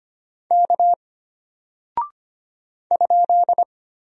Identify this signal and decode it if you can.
Morse code, heard as a keyed tone, about 25 wpm, 700 Hz. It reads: K ?
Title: Single Character (Letter:Number:Punctuation:Prosign) Alt 25wpm